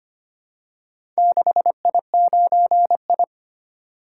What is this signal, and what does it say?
Morse code, heard as a keyed tone, about 25 wpm, 700 Hz. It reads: 6I9I
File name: practice / 25wpm / Four Letter-Number Combinations - 25wpm